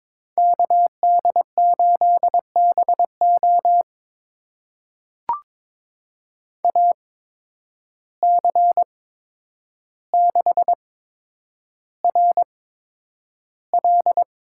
Morse code, heard as a keyed tone, about 22 wpm, spaced as for 9 wpm, 700 Hz. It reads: KD8BO AC6RL
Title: USA Call Signs - 5 Characters - 8x Character Spacing - 22wpm